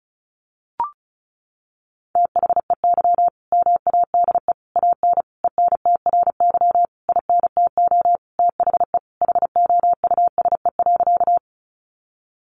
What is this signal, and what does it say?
Morse code, heard as a keyed tone, about 34 wpm, 700 Hz. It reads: THEY MADE AN ENTRY INTO THE HOUSE.
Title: Sentences from Top 1500 Words 35wpm